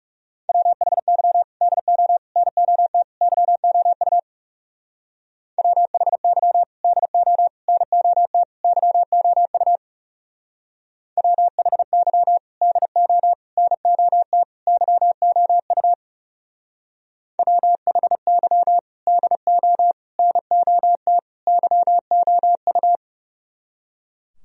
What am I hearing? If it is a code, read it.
Morse code, heard as a keyed tone, about 37 wpm, 700 Hz. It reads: WHY DO NOT YOU WHY DO NOT YOU WHY DO NOT YOU WHY DO NOT YOU
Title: Speed-Racing - Sets of 4 Words 30wpm